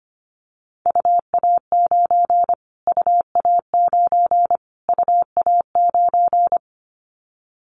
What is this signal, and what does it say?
Morse code, heard as a keyed tone, about 25 wpm, 700 Hz. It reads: UA9 UA9 UA9
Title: Callsign Prefixes for Top 25 DXCC - Repeated 3x - 25wpm